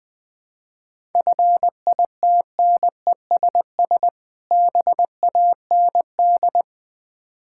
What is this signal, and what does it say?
Morse code, heard as a keyed tone, about 20 wpm, 700 Hz. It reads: FITNESS BAND
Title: Difficult Words with Hints - 20wpm